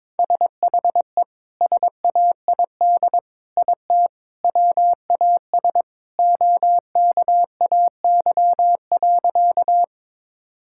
Morse code, heard as a keyed tone, about 22 wpm, 700 Hz. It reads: SHE SAID IT WAS OKAY.
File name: practice / 22wpm / Sentences with 2-Word Phrases - 22wpm